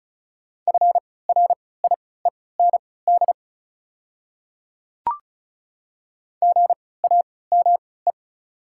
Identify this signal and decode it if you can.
Morse code, heard as a keyed tone, about 36 wpm, spaced as for 20 wpm, 700 Hz. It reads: FRIEND GAME
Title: 3x Character Spacing - Top 500 Words - 35wpm